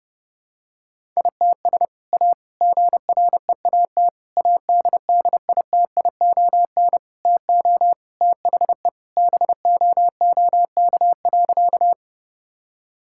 Morse code, heard as a keyed tone, about 30 wpm, 700 Hz. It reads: ITS A GREAT ADDITION TO THE BOOK.
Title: Sentences from Top 1000 Words - Review Part 6 - 30wpm